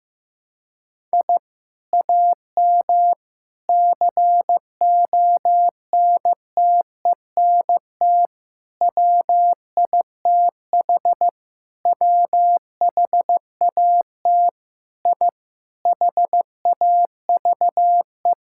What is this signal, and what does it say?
Morse code, heard as a keyed tone, about 15 wpm, 700 Hz. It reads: I AM CONTENT WITH WHAT I HAVE
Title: Sentences from Top 1000 Words - Review Part 6 - 15wpm